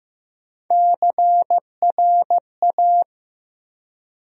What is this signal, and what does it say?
Morse code, heard as a keyed tone, about 15 wpm, 700 Hz. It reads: CRA
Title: USA Call Sign Postfixes - 3 Letters 15wpm